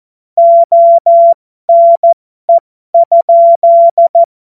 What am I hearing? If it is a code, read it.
Morse code, heard as a keyed tone, about 14 wpm, spaced as for 12 wpm, 680 Hz. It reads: ONE?
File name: Morsekurs / Witze